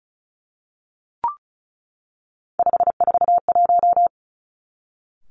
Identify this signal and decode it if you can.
Morse code, heard as a keyed tone, about 36 wpm, 700 Hz. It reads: 541
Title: Three Challenging Letter-Number Combinations 35wpm